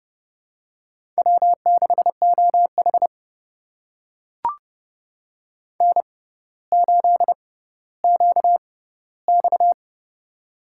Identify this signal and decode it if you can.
Morse code, heard as a keyed tone, about 31 wpm, spaced as for 15 wpm, 700 Hz. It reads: W6OH N8QX
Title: USA Call Signs - 4 Characters - 6x Character Spacing - 30wpm